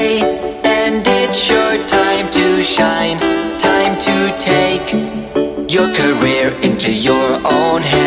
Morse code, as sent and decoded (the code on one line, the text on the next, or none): none